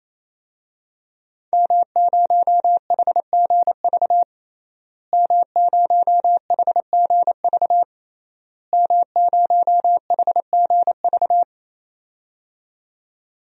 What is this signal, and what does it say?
Morse code, heard as a keyed tone, about 28 wpm, 700 Hz. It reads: M0HGV M0HGV M0HGV
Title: DXCC-100 Call Signs - 5 Characters - Repeated 3x - 28wpm